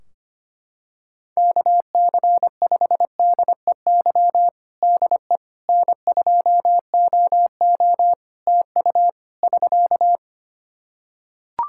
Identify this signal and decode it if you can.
Morse code, heard as a keyed tone, about 25 wpm, 700 Hz. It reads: KC5DEY DE N2OO TU <SK>
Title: K1USN SST QSO Practice 25wpm